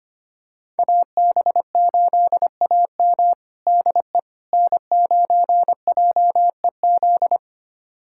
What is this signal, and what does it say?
Morse code, heard as a keyed tone, about 25 wpm, 700 Hz. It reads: AB8AM DE N9JEZ
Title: USA Callsign DE Callsign - 5 Characters - 25wpm